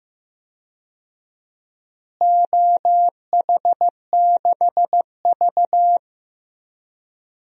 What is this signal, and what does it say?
Morse code, heard as a keyed tone, about 16 wpm, 700 Hz. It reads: OH6V
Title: DXCC-100 Call Signs - 4 Characters - Repeated 2x - 15wpm